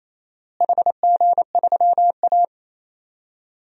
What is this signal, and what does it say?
Morse code, heard as a keyed tone, about 27 wpm, 700 Hz. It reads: HG3A
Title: DXCC-100 Call Signs - 4 Characters - Repeated 3x - 28wpm